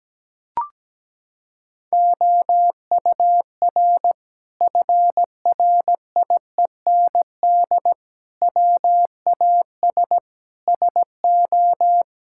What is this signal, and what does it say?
Morse code, heard as a keyed tone, about 17 wpm, 700 Hz. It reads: OUR FRIEND WAS SO
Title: Sentences from Top 1000 Words - Review Part 5 - 17wpm